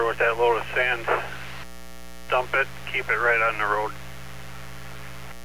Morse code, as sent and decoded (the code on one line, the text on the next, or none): none